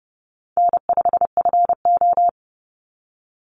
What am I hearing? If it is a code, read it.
Morse code, heard as a keyed tone, about 30 wpm, 700 Hz. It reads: N5FO
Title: Easy Call Signs Repeated 3x - 30wpm